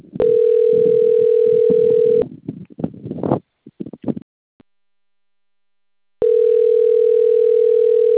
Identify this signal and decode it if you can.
Ringback tone (North America), 440 + 480 Hz.
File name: rg-1600-unknown- - - .92969